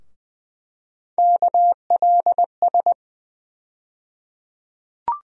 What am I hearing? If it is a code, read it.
Morse code, heard as a keyed tone, about 20 wpm, 700 Hz. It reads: KLS
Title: USA Call Sign Postfixes - 3 Letters 20wpm